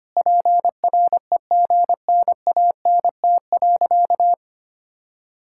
Morse code, heard as a keyed tone, about 25 wpm, 700 Hz. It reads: PREGNANT.
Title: Sentences from Top 1000 Words - Review Part 6 - 25wpm